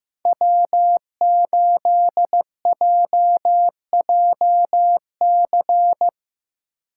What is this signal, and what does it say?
Morse code, heard as a keyed tone, about 15 wpm, 700 Hz. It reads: W8JJC
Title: USA Call Signs - 3x Character Spacing - 15wpm